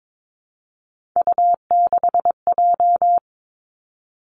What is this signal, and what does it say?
Morse code, heard as a keyed tone, about 22 wpm, 700 Hz. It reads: U6J
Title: Three Challenging Letter-Number Combinations 22wpm